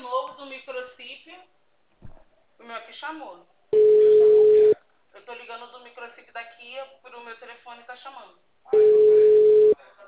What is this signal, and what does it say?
Ringback tone (Europe, 425 Hz), with speech.